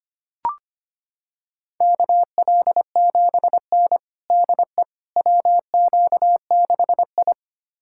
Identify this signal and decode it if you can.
Morse code, heard as a keyed tone, about 25 wpm, 700 Hz. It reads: KL7N DE WQ6I
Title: USA Callsign DE Callsign - 4 Characters - 25wpm